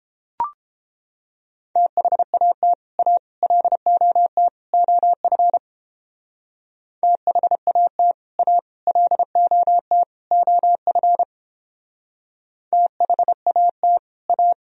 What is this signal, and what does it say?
Morse code, heard as a keyed tone, about 29 wpm, 700 Hz. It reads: THAT A LOT OF THAT A LOT OF THAT A